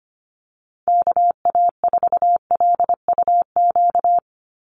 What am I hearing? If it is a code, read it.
Morse code, heard as a keyed tone, about 25 wpm, 700 Hz. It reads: KA4LUQ
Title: USA Call Signs 25wpm